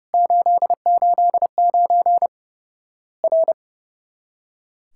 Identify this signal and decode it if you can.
Morse code, heard as a keyed tone, about 30 wpm, 700 Hz. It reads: 889 R
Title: Worked All Europe DX Contest - QTC Exchange - 30wpm